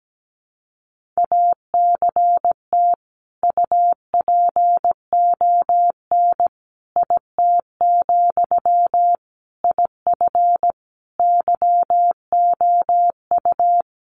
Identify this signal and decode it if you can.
Morse code, heard as a keyed tone, about 17 wpm, 700 Hz. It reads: ACT UPON IT, IF YOU
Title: Sentences from Top 1100 Words 17wpm